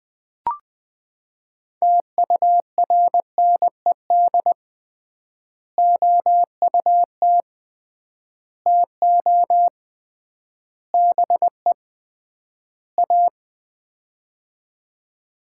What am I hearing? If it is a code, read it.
Morse code, heard as a keyed tone, about 20 wpm, 700 Hz. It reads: TURNED OUT TO BE A
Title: 3x Word-Spacing - Sets of 5 Words - 20wpm